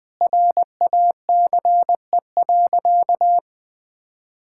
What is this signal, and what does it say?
Morse code, heard as a keyed tone, about 20 wpm, 700 Hz. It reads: RACE.
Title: Sentences with 8-Word Phrases - Spoken Beginning - 20wpm